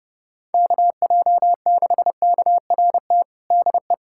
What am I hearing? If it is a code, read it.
Morse code, heard as a keyed tone, about 30 wpm, 700 Hz. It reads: KJ6KRT DE
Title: USA Callsign DE Callsign 30wpm